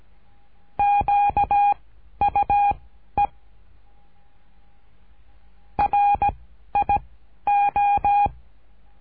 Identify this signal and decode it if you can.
Morse code, heard as a keyed tone, about 17 wpm, spaced as for 11 wpm, 810 Hz. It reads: QUE RIO
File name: cursos / CURSO DE TELEGRAFIA / Telegrafia - Trilha 13